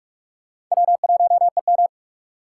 Morse code, heard as a keyed tone, about 44 wpm, 700 Hz. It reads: W1EM